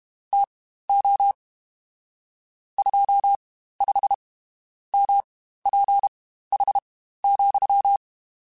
Morse code, heard as a keyed tone, about 32 wpm, spaced as for 14 wpm, 800 Hz. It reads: TO 25MPH,
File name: Spaced Characters / Spaced Characters-mp3s / Weather 5 5 sp @30